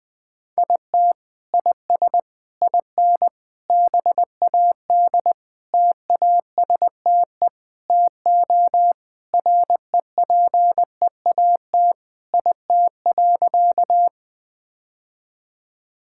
Morse code, heard as a keyed tone, about 20 wpm, 700 Hz. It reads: IT IS IN BAD TASTE TO REPEAT IT.
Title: Sentences from Top 1600 Words 20wpm